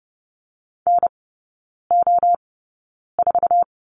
Morse code, heard as a keyed tone, about 30 wpm, spaced as for 9 wpm, 700 Hz. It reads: NO4
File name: 081 Lesson 14e - Number 2 - Three Characters 30wpm